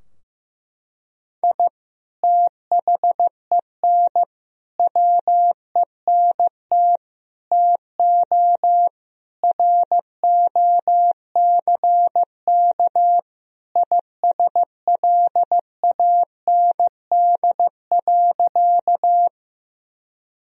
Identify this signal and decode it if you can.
Morse code, heard as a keyed tone, about 16 wpm, 700 Hz. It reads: I THEN WENT TO ROCK ISLAND.
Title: Sentences from Top 1000 Words - Review Part 3 - 15wpm